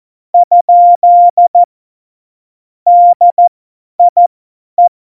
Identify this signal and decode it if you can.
Morse code, heard as a keyed tone, about 14 wpm, spaced as for 10 wpm, 700 Hz. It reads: ? DIE
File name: Morsekurs / Witze / Witz 09. Oktober 2018, 50 F70